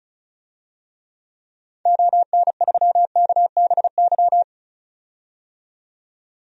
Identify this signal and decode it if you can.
Morse code, heard as a keyed tone, about 36 wpm, 700 Hz. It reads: ON3KBY
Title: DXCC-100 Call Signs - 6 Characters - 35wpm